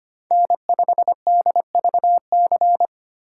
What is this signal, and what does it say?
Morse code, heard as a keyed tone, about 25 wpm, 700 Hz. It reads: N5DVC
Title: USA Call Signs - 2x Character Spacing - 25wpm